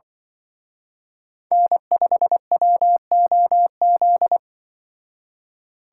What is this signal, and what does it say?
Morse code, heard as a keyed tone, about 25 wpm, 700 Hz. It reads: N5WOZ
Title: Speed Racing - USA 1x3 Call Signs 20wpm